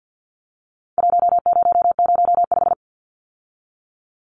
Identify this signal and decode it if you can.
Morse code, heard as a keyed tone, about 50 wpm, 700 Hz. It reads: 1005
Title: Numbers - Contest Style - 4-Digit 50wpm